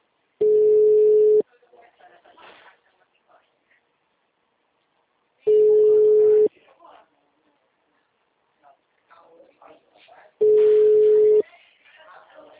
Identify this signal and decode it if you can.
Ringback tone (Europe, 425 Hz), with speech.